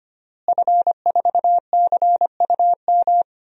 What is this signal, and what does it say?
Morse code, heard as a keyed tone, about 25 wpm, 700 Hz. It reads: F4CUM
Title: DXCC-100 Call Signs - 5 Characters - Repeated 2x - 25wpm